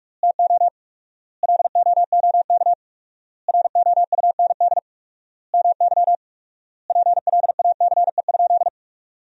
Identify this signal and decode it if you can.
Morse code, heard as a keyed tone, about 46 wpm, 700 Hz. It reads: TO LOOK ROUND MY PLACE?
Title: 4x Word-Spacing - Sentences from Top 200 Words - 45wpm